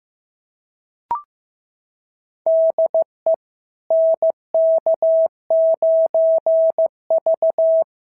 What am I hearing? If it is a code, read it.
Morse code, heard as a keyed tone, about 15 wpm, 650 Hz. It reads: DE NK9V
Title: SOTA QSO Practice 15wpm